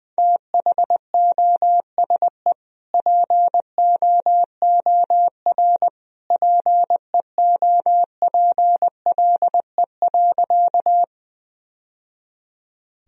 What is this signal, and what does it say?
Morse code, heard as a keyed tone, about 20 wpm, 700 Hz. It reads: THOSE POOR PEOPLE.
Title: Sentences from Top 1000 Words - Review Part 1 - 20wpm